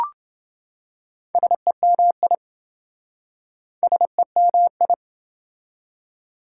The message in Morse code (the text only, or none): SEMI SEMI